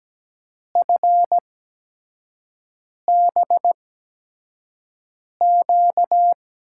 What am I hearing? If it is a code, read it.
Morse code, heard as a keyed tone, about 17 wpm, spaced as for 5 wpm, 700 Hz. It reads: FBQ
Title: USA Call Signs - 6 Characters - 8x Character Spacing - 17wpm